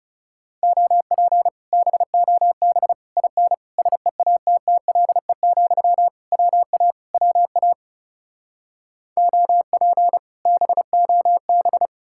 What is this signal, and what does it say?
Morse code, heard as a keyed tone, about 37 wpm, 700 Hz. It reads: OP BOB IN SEATTLE, WA WA OP BOB
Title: Speed-Racing Common QSO Phrases Encore - 30wpm